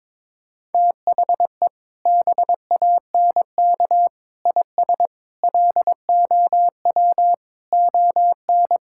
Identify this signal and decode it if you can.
Morse code, heard as a keyed tone, about 22 wpm, 700 Hz. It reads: THE BANK IS LOW ON
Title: Sentences from Top 1000 Words - Review Part 5 - 22wpm